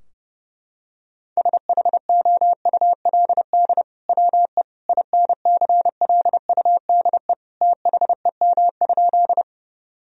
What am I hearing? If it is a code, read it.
Morse code, heard as a keyed tone, about 30 wpm, 700 Hz. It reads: SHOULD WE INCLUDE THEM?
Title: Sentences from Top 1000 Words - Review Part 3 - 30wpm